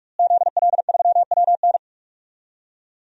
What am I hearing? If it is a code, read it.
Morse code, heard as a keyed tone, about 42 wpm, 700 Hz. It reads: ZL3WN
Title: DXCC-100 Call Signs - 5 Characters - Repeated 3x - 45wpm